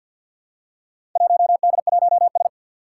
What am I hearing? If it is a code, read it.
Morse code, heard as a keyed tone, about 53 wpm, 700 Hz. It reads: 1D1S
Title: Four Challenging Letter-Number Combinations 50wpm